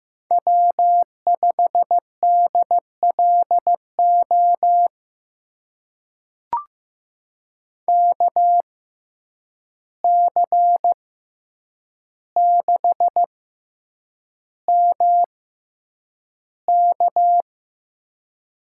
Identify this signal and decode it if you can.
Morse code, heard as a keyed tone, about 15 wpm, spaced as for 8 wpm, 700 Hz. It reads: W5DLO KC6MK
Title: USA Call Signs - 6x Character Spacing - 15wpm